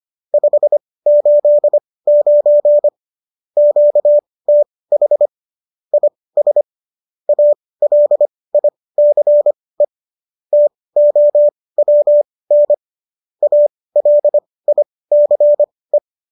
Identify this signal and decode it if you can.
Morse code, heard as a keyed tone, about 25 wpm, spaced as for 18 wpm, 590 Hz. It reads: 589 QTH IS ALICE TOWN ALICE
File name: qso204 18